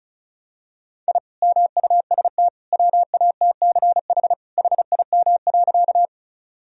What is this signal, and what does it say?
Morse code, heard as a keyed tone, about 36 wpm, 700 Hz. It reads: I MUST WATCH HIM.
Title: Sentences from Top 1000 Words - Review Part 1 - 35wpm